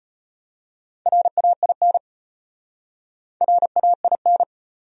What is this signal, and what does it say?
Morse code, heard as a keyed tone, about 33 wpm, 700 Hz. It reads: RAIN RAIN